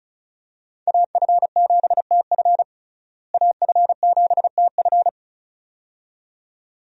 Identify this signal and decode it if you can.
Morse code, heard as a keyed tone, about 36 wpm, 700 Hz. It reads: AF7TF AF7TF
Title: USA Call Signs - 5 Characters - Repeated 2x - 35wpm